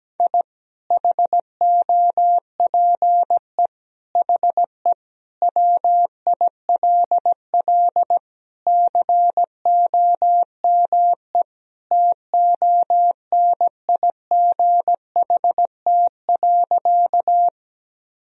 Morse code, heard as a keyed tone, about 17 wpm, 700 Hz. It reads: I HOPE HE WILL COME TONIGHT.
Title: Sentences from Top 1000 Words - Review Part 3 - 17wpm